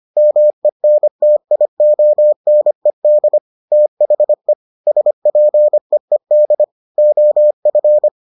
Morse code, heard as a keyed tone, about 25 wpm, 600 Hz. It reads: MENTIONED THE SPEED OF